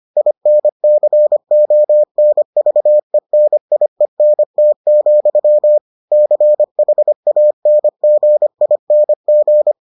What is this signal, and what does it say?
Morse code, heard as a keyed tone, about 25 wpm, 600 Hz. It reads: INCONVENIENT, CHANGING